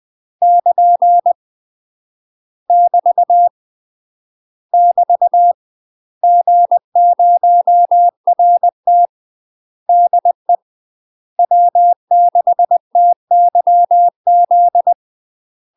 Morse code, heard as a keyed tone, about 20 wpm, 700 Hz. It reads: <KN> <BT> <BT> G0RT DE W6TYZ
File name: UK MASTER QSOs TRAINING/20WPM